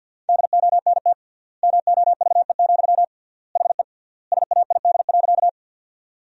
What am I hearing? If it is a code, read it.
Morse code, heard as a keyed tone, about 50 wpm, 700 Hz. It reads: DONT MOVE, HE SAID.
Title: 3x Word-Spacing - Sentences from Top 300 Words - 50wpm